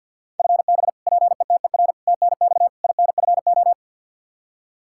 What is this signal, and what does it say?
Morse code, heard as a keyed tone, about 48 wpm, 700 Hz. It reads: FB PETER TNX INFO
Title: Practice QSO - Exchange 3 - More Info - 50wpm